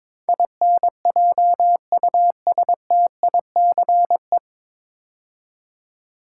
Morse code, heard as a keyed tone, about 22 wpm, 700 Hz. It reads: INJUSTICE
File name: 9 Letter Words - First 3 Letters Given - 22wpm